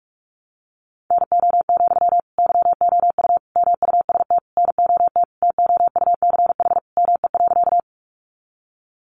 Morse code, heard as a keyed tone, about 44 wpm, 700 Hz. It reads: NO, YOU MUST NOT TOUCH ME.
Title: Sentences from Top 1200 Words 45wpm